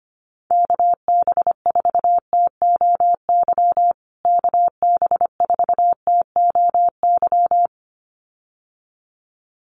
Morse code, heard as a keyed tone, about 25 wpm, 700 Hz. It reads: KB4TOY KB4TOY
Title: USA Call Signs Repeated 2x 25wpm